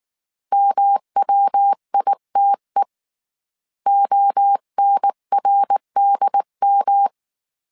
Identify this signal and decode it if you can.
Morse code, heard as a keyed tone, about 19 wpm, 800 Hz. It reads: MWITE ONRDM